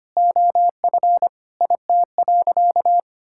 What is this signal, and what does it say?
Morse code, heard as a keyed tone, about 25 wpm, 700 Hz. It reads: OF IT.
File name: practice / 25wpm / All-Age Appropriate Jokes - 25wpm